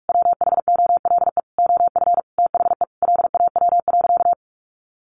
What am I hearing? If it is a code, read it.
Morse code, heard as a keyed tone, about 44 wpm, 700 Hz. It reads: WHOLE OF THE LAW.
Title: Sentences from Top 1400 Words 45wpm